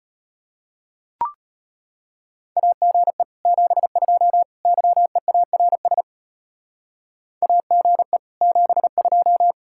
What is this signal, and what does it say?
Morse code, heard as a keyed tone, about 30 wpm, 700 Hz. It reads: AGE 72 YEARS AGE 72
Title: Speed-Racing Common QSO Phrases - 25wpm